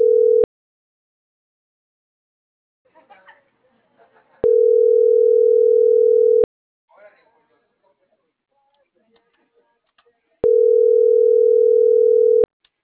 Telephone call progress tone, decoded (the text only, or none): ringback tone